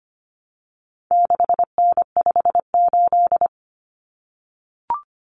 Morse code, heard as a keyed tone, about 25 wpm, 700 Hz. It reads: 6N58
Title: Common Cut Numbers - 4-Digit 25wpm